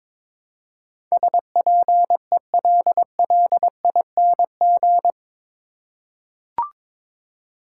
Morse code, heard as a keyed tone, about 22 wpm, 700 Hz. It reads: SPELLING
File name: practice / 22wpm / Eight Letter Words - 22wpm